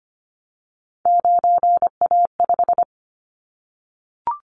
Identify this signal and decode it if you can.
Morse code, heard as a keyed tone, about 25 wpm, 700 Hz. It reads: 9A5
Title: Callsign Prefixes for Top 50 DXCC - 25wpm